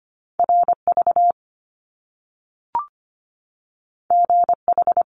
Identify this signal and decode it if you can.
Morse code, heard as a keyed tone, about 25 wpm, 700 Hz. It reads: RV GH
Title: Top Two Letter Combinations in Words 25wpm